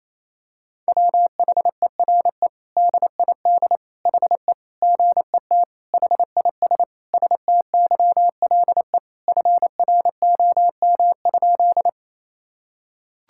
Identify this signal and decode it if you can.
Morse code, heard as a keyed tone, about 27 wpm, 700 Hz. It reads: WHERE DID HE GET HIS STYLE FROM?
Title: Sentences from Top 1000 Words - Review Part 5 - 28wpm